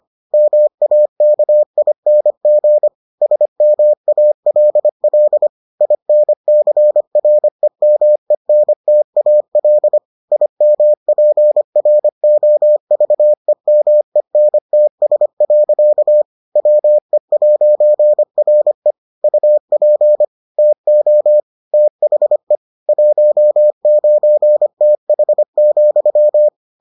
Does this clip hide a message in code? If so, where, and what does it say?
Yes: Morse code from 0.3 to 26.5 s, heard as a keyed tone, about 25 wpm, 600 Hz. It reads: MAKING SMALL INCREMENTAL IMPROVEMENTS. WE'RE UP TO THE 19TH,